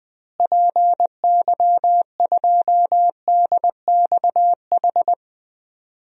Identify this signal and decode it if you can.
Morse code, heard as a keyed tone, about 20 wpm, 700 Hz. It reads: PY2DXH
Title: DXCC-100 Call Signs - 6 Characters - Repeated 3x - 20wpm